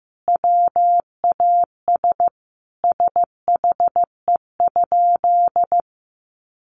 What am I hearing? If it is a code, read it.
Morse code, heard as a keyed tone, about 15 wpm, 700 Hz. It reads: WAS SHE?
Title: Sentences from Top 100 Words 15wpm